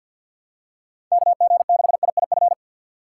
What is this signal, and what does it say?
Morse code, heard as a keyed tone, about 51 wpm, 700 Hz. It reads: KG6IIF
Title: USA Call Signs - 6 Characters - 3x Character Spacing - 50wpm